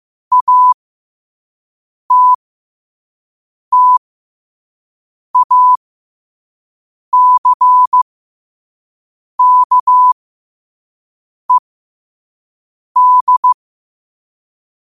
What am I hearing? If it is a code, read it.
Morse code, heard as a keyed tone, about 15 wpm, spaced as for 5 wpm, 1000 Hz. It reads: ATTACKED
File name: qotdfiles / 5wpm